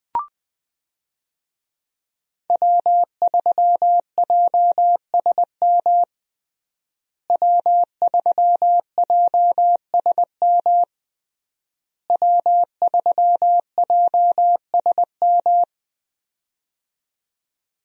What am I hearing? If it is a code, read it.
Morse code, heard as a keyed tone, about 20 wpm, 700 Hz. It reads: W3JSM W3JSM W3JSM
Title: USA Call Signs - 5 Characters - Repeated 3x - 20wpm